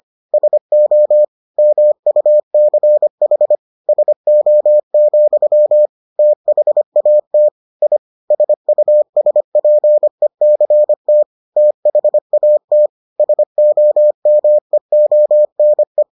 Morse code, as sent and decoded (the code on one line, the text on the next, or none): ... --- / -- ..- -.-. .... / ... --- --..-- / - .... .- - / .. / ... ..- ... .--. -.-. - / - .... .- - / ... --- -- . --- -. .
SO MUCH SO, THAT I SUSPECT THAT SOMEONE